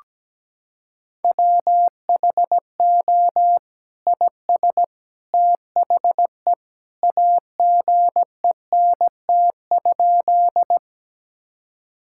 Morse code, heard as a keyed tone, about 17 wpm, 700 Hz. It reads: WHO IS THE AGENT?
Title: Sentences from Top 1600 Words 17wpm